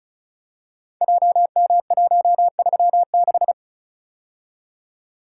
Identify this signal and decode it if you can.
Morse code, heard as a keyed tone, about 36 wpm, 700 Hz. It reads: JM136
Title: Code Groups - 5 Letter:Numbers - 35wpm